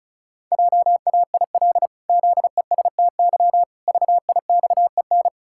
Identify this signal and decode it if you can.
Morse code, heard as a keyed tone, about 36 wpm, 700 Hz. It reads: JAIL ZESTY VIXEN
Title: Pangram Challenge v2 - 35wpm